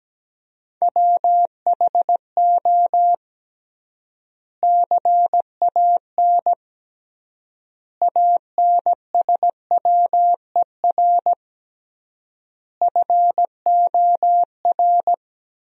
Morse code, heard as a keyed tone, about 17 wpm, 700 Hz. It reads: WHO CAN ANSWER FOR
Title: 3x Word-Spacing - Sentences from Top 300 Words - 17wpm